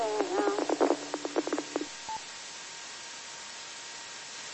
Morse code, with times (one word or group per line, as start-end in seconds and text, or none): none